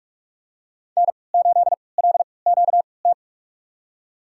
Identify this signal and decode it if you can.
Morse code, heard as a keyed tone, about 46 wpm, spaced as for 25 wpm, 700 Hz. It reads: N8LQT
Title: USA Call Signs - 5 Characters - 3x Character Spacing - 45wpm